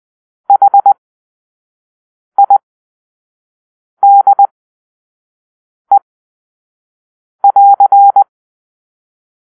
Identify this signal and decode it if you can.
Morse code, heard as a keyed tone, about 20 wpm, spaced as for 5 wpm, 800 Hz. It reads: HIDE<AR>
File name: arrl / Grp-2 Drill-2